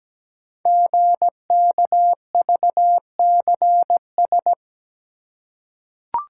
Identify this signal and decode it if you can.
Morse code, heard as a keyed tone, about 17 wpm, 700 Hz. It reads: GKVCS
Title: Code Groups - 5 Letters - 17wpm